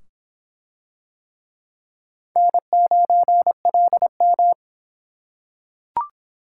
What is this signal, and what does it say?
Morse code, heard as a keyed tone, about 24 wpm, 700 Hz. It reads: N9LM